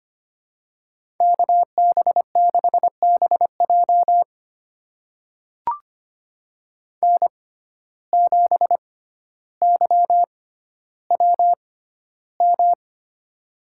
Morse code, heard as a keyed tone, about 25 wpm, spaced as for 13 wpm, 700 Hz. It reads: KB6BJ N7YWM
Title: USA Call Signs - 6x Character Spacing - 25wpm